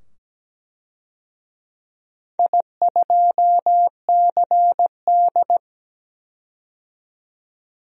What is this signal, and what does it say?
Morse code, heard as a keyed tone, about 17 wpm, 700 Hz. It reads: I2CD
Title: DXCC-100 Call Signs - 4 Characters - 17wpm